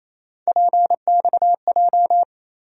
Morse code, heard as a keyed tone, about 28 wpm, 700 Hz. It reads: PXJ
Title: USA Call Sign Postfixes - 3 Letters 28wpm